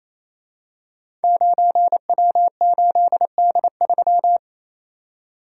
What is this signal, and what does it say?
Morse code, heard as a keyed tone, about 27 wpm, 700 Hz. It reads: 9W8D3
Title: Code Groups - 5 Letter:Numbers - 28wpm